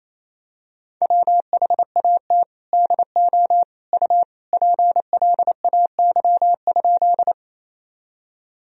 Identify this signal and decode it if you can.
Morse code, heard as a keyed tone, about 28 wpm, 700 Hz. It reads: WHAT DO U PLAY?
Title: Sentences from Top 200 Words and Abbreviations 28wpm